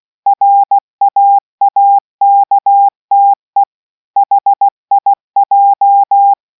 Morse code, heard as a keyed tone, about 16 wpm, 800 Hz. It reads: RAAKTE HIJ